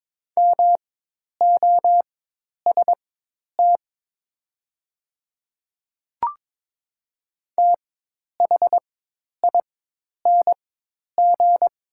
Morse code, heard as a keyed tone, about 22 wpm, spaced as for 10 wpm, 700 Hz. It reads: MOST THING
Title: 4x Character Spacing - Top 100 Words - 22wpm